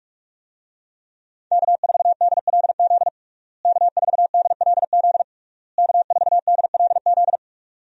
Morse code, heard as a keyed tone, about 46 wpm, 700 Hz. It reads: K4DLZ K4DLZ K4DLZ
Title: USA Call Signs - 5 Characters - Repeated 3x - 45wpm